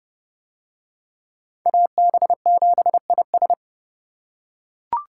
Morse code, heard as a keyed tone, about 29 wpm, 700 Hz. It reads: AB7IS